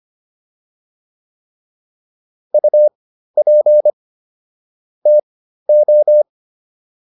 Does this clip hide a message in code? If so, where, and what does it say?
Yes: Morse code from 2.5 to 6.2 s, heard as a keyed tone, about 25 wpm, spaced as for 13 wpm, 600 Hz. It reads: UP TO